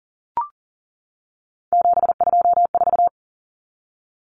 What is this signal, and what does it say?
Morse code, heard as a keyed tone, about 40 wpm, 700 Hz. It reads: 724
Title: Numbers - 3-Digit 40wpm